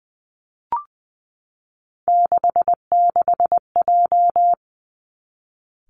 Morse code, heard as a keyed tone, about 20 wpm, 700 Hz. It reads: 66J
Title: Three Challenging Letter-Number Combinations 20wpm